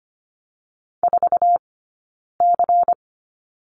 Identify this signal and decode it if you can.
Morse code, heard as a keyed tone, about 25 wpm, 700 Hz. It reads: 4 C
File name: course / 25:10wpm / 092 Lesson 16d - Letter c - Two Characters 25wpm